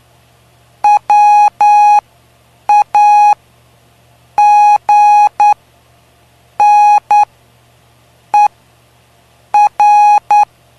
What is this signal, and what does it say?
Morse code, heard as a keyed tone, about 9 wpm, spaced as for 6 wpm, 830 Hz. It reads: WAGNER
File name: cw / py2lcd / 05-PY2LCD-Li��o 5